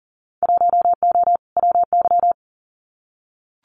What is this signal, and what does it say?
Morse code, heard as a keyed tone, about 40 wpm, 700 Hz. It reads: 1O WY
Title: Winter Field Day Exchanges Encore 40wpm